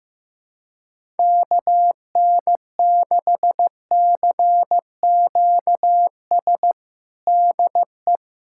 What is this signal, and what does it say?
Morse code, heard as a keyed tone, about 15 wpm, 700 Hz. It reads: KN6CQS DE
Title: USA Callsign DE Callsign - 6 Characters - 15wpm